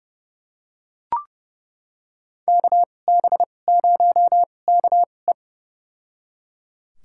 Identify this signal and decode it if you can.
Morse code, heard as a keyed tone, about 32 wpm, spaced as for 22 wpm, 700 Hz. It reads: KB0KE
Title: USA Call Signs - 5 Characters - 2x Character Spacing - 30wpm